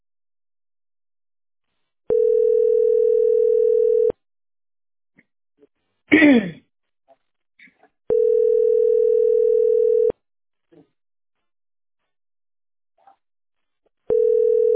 Ringback tone (North America), 440 + 480 Hz.